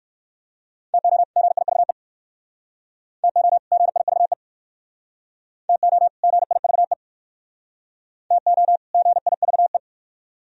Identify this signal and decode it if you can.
Morse code, heard as a keyed tone, about 56 wpm, 700 Hz. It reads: TO GIVE TO GIVE TO GIVE TO GIVE
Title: Speed-Racing - Sets of 2 Words 45wpm